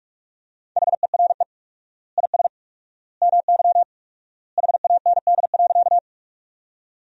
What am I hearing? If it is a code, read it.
Morse code, heard as a keyed tone, about 41 wpm, 700 Hz. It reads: HERE IS MY HAND.
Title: 4x Word-Spacing - Sentences from Top 200 Words - 45wpm